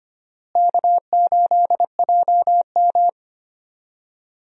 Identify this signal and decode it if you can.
Morse code, heard as a keyed tone, about 25 wpm, 700 Hz. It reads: K8JM
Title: USA Call Signs - 4 Characters - Repeated 3x - 25wpm